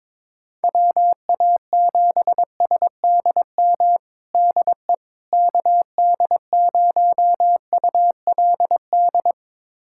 Morse code, heard as a keyed tone, about 22 wpm, 700 Hz. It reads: WA7SDM DE KD0ULD
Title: USA Callsign DE Callsign - 6 Characters - 22wpm